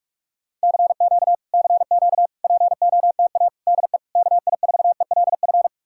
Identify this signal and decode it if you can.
Morse code, heard as a keyed tone, about 59 wpm, spaced as for 52 wpm, 700 Hz. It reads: CQ CQ POTA DE KI4ELF
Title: Speed-Racing Common QSO Phrases Encore - 45wpm